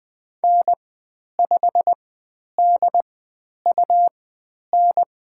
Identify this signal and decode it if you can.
Morse code, heard as a keyed tone, about 20 wpm, spaced as for 10 wpm, 700 Hz. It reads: N5DUN
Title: 102 Lesson 17g - Letter u - Call Signs 20wpm